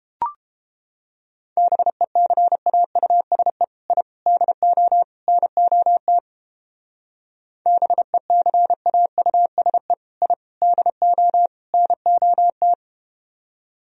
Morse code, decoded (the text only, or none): BECAUSE I DO NOT BECAUSE I DO NOT